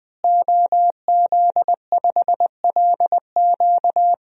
Morse code, heard as a keyed tone, about 20 wpm, 700 Hz. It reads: OZ5LQ
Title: DXCC-100 Call Signs - 5 Characters - Repeated 3x - 20wpm